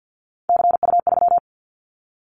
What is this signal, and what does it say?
Morse code, heard as a keyed tone, about 50 wpm, 700 Hz. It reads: CU3
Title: Callsign Prefixes for Top 50 DXCC - 50wpm